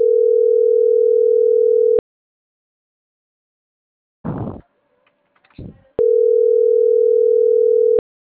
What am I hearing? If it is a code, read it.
Ringback tone (North America), 440 + 480 Hz.